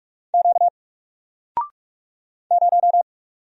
Morse code, heard as a keyed tone, about 45 wpm, 700 Hz. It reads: Q 0
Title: Single Letter-Number Alt 45wpm